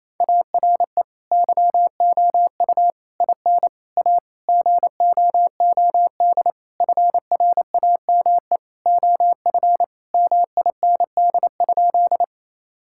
Morse code, heard as a keyed tone, about 28 wpm, 700 Hz. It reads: ARE YOU IN A GOOD FRAME OF MIND?
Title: Sentences from Top 1000 Words - Review Part 6 - 28wpm